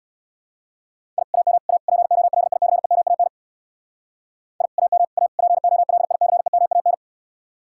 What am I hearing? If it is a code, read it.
Morse code, heard as a keyed tone, about 64 wpm, 700 Hz. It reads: I AM A POLICEMAN I AM A POLICEMAN